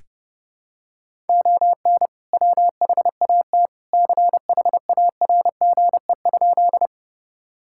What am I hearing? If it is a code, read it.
Morse code, heard as a keyed tone, about 30 wpm, 700 Hz. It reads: ON WHAT CHARGE?
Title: Sentences from Top 1000 Words - Review Part 6 - 30wpm